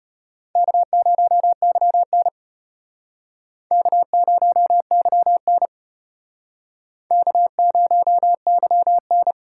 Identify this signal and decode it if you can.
Morse code, heard as a keyed tone, about 32 wpm, 700 Hz. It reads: K0YN K0YN K0YN